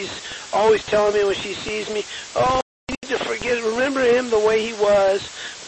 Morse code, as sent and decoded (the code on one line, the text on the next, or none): none